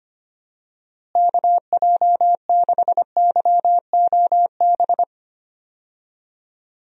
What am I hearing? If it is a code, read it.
Morse code, heard as a keyed tone, about 25 wpm, 700 Hz. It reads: KJ6YOB
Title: USA Call Signs - 6 Characters - Repeated 2x - 25wpm